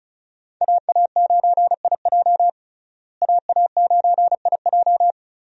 Morse code, heard as a keyed tone, about 36 wpm, 700 Hz. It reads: AA9IJ AA9IJ
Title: USA Call Signs - 5 Characters - Repeated 2x - 35wpm